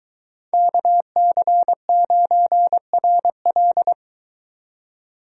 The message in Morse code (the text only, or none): KC9RL